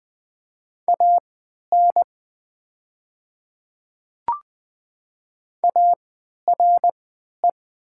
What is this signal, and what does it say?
Morse code, heard as a keyed tone, about 20 wpm, spaced as for 11 wpm, 700 Hz. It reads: AN ARE
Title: 3x Character Spacing - Top 100 Words - 20wpm